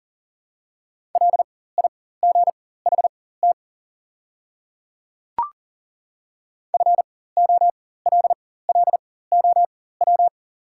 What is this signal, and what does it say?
Morse code, heard as a keyed tone, about 40 wpm, spaced as for 19 wpm, 700 Hz. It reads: LIGHT FOLLOW
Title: 4x Character Spacing - Top 300 Words - 40wpm